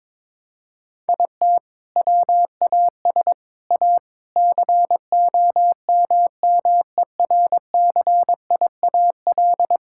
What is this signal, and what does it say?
Morse code, heard as a keyed tone, about 22 wpm, 700 Hz. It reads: IT WAS A COMMERCIAL
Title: Sentences from Top 1000 Words - Review Part 5 - 22wpm